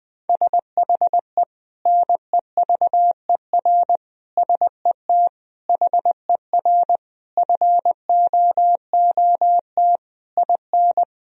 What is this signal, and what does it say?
Morse code, heard as a keyed tone, about 20 wpm, 700 Hz. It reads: SHE NEVER SET HER FOOT IN
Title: Sentences from Top 1000 Words - Review Part 3 - 20wpm